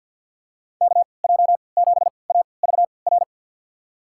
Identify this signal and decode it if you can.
Morse code, heard as a keyed tone, about 49 wpm, spaced as for 28 wpm, 700 Hz. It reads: KJ7AVR
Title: USA Call Signs - 6 Characters - 3x Character Spacing - 50wpm